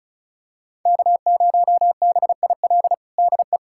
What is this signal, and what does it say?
Morse code, heard as a keyed tone, about 34 wpm, 700 Hz. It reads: K0BIL DE
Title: USA Callsign DE Callsign - 5 Characters - 35wpm